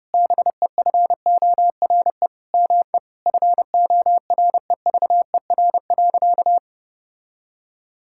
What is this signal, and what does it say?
Morse code, heard as a keyed tone, about 30 wpm, 700 Hz. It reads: BEFORE ME FOREVER.
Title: Sentences from Top 1600 Words 30wpm